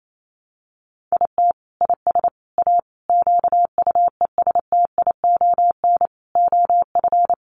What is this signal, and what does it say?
Morse code, heard as a keyed tone, about 28 wpm, 700 Hz. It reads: IT IS A QUESTION OF
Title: Sentences from Top 1200 Words 28wpm